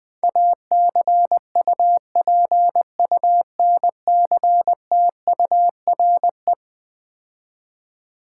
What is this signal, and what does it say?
Morse code, heard as a keyed tone, about 20 wpm, 700 Hz. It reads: ACUPUNCTURE
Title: Difficult Words - Speed-Racing with Hints - 20wpm